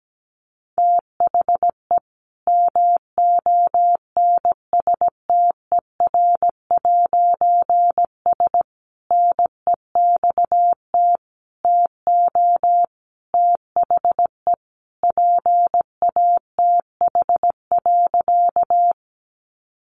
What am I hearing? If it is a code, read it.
Morse code, heard as a keyed tone, about 17 wpm, 700 Hz. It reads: THE MONSTER'S NEXT TO THE PATH.
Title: A Sound Of Thunder 17wpm